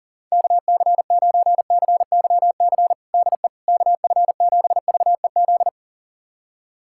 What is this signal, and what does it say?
Morse code, heard as a keyed tone, about 40 wpm, 700 Hz. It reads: KC9CYC DE KF7VEZ
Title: USA Callsign DE Callsign 40wpm